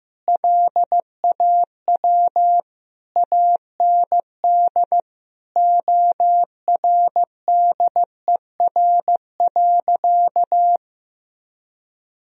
Morse code, heard as a keyed tone, about 16 wpm, 700 Hz. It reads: LAW AND ORDER.
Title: Sentences from Top 1000 Words - Review Part 3 - 15wpm